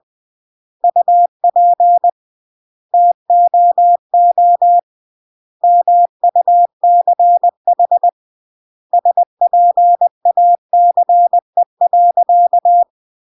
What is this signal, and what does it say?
Morse code, heard as a keyed tone, about 20 wpm, 700 Hz. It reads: UP TOO MUCH SPACE.